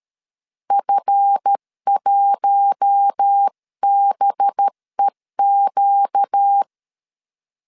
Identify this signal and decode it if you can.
Morse code, heard as a keyed tone, about 13 wpm, 800 Hz. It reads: F1BEQ
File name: LEZIONI / 10 SIGNS / 10B INT 12wpm 2414